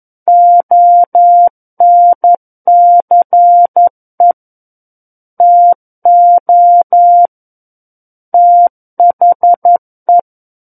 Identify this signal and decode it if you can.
Morse code, heard as a keyed tone, about 11 wpm, 700 Hz. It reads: ONCE TO THE
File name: 11 WPM